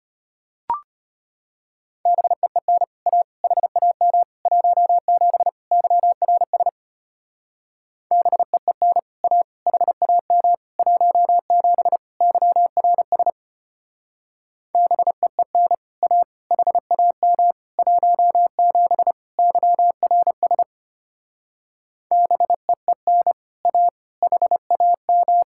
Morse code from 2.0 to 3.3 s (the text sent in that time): BEEN A